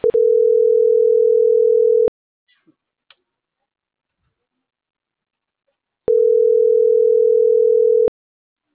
Ringback tone (North America), 440 + 480 Hz.